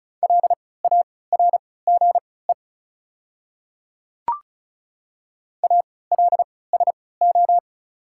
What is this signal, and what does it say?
Morse code, heard as a keyed tone, about 34 wpm, spaced as for 20 wpm, 700 Hz. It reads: LARGE ALSO
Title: 3x Character Spacing - Top 200 Words - 35wpm